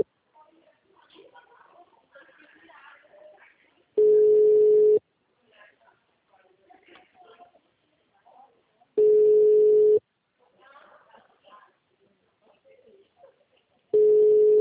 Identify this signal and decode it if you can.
Ringback tone (Europe, 425 Hz), with speech.